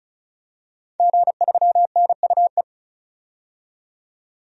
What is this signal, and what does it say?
Morse code, heard as a keyed tone, about 34 wpm, 700 Hz. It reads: G3NUE
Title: DXCC-100 Call Signs - 5 Characters - 35wpm